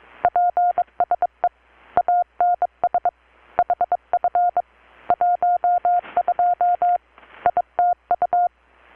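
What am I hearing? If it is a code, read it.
Morse code, heard as a keyed tone, about 23 wpm, 720 Hz. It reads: PSE ANS HF 12 ITU
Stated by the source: Morse from KPH's Night of Nights XX maritime radiotelegraphy event, received in Chiapas, Mexico about 3,800 km away on a Kenwood R-600 with a grounded 10 m long-wire antenna.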